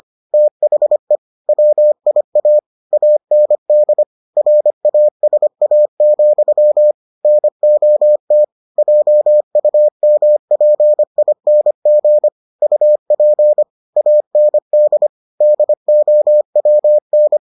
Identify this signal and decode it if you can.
Morse code, heard as a keyed tone, about 25 wpm, 600 Hz. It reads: THE WIA AND RASA, NOT JUMPING UP AND DOWN